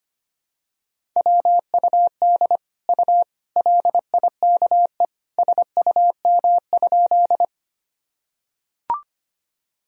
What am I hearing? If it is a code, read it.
Morse code, heard as a keyed tone, about 25 wpm, 700 Hz. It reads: WUD U LIKE SUM?
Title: Sentences from Top 100 Words and Abbreviations 25wpm